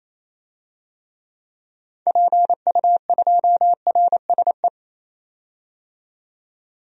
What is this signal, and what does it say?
Morse code, heard as a keyed tone, about 27 wpm, 700 Hz. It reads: PU2RSE
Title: DXCC-100 Call Signs - 6 Characters - 28wpm